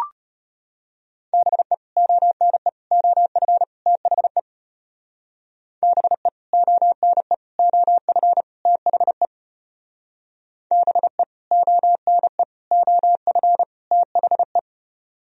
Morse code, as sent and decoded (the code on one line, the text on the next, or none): -... / --- -. / --- ..-. / - .... / -... / --- -. / --- ..-. / - .... / -... / --- -. / --- ..-. / - .... .
BE ONE OF THE BE ONE OF THE BE ONE OF THE